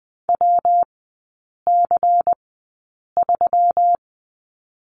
Morse code, heard as a keyed tone, about 20 wpm, spaced as for 8 wpm, 690 Hz. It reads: WC3
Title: 145 Lesson 24e - Letter y - Three Characters 20wpm